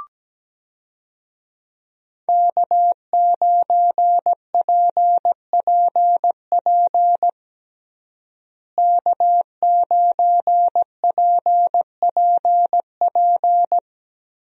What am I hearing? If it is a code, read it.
Morse code, heard as a keyed tone, about 17 wpm, 700 Hz. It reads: K9PPP K9PPP
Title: USA Call Signs - 5 Characters - Repeated 3x - 17wpm